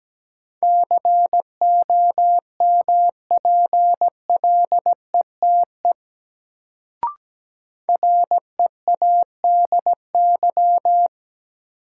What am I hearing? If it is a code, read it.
Morse code, heard as a keyed tone, about 17 wpm, 700 Hz. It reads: COMPLETE READY
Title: Sentences from Top 400 Words - New Word Drill - 17wpm